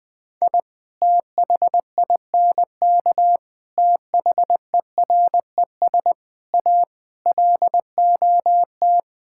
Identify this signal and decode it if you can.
Morse code, heard as a keyed tone, about 21 wpm, 700 Hz. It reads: I THINK THERES A LOT
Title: Sets of 6 Words - 20wpm